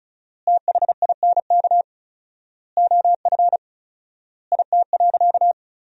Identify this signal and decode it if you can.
Morse code, heard as a keyed tone, about 36 wpm, 700 Hz. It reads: THINK OF IT.
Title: 4x Word-Spacing - Sentences from Top 200 Words - 35wpm